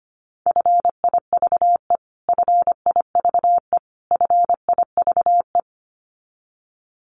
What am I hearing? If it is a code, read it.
Morse code, heard as a keyed tone, about 25 wpm, 700 Hz. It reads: FIVE FIVE FIVE
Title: Top 500 Words Repeated 25wpm